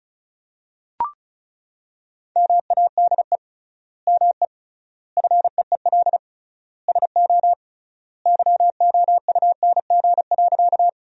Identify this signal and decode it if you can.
Morse code, heard as a keyed tone, about 36 wpm, 700 Hz. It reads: MADE ME FEEL SO YOUNG.
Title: 3x Word-Spacing - Sentences from Top 400 Words - 35wpm